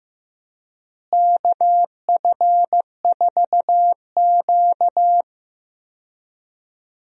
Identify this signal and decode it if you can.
Morse code, heard as a keyed tone, about 15 wpm, 700 Hz. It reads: KF4Q
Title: USA Call Signs - 4 Characters - Repeated 2x - 15wpm